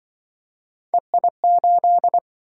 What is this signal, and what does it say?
Morse code, heard as a keyed tone, about 27 wpm, 700 Hz. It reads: EI8